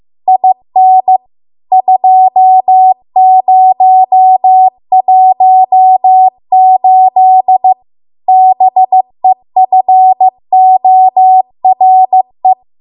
Morse code, heard as a keyed tone, about 15 wpm, 750 Hz. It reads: IN 2018 BEFORE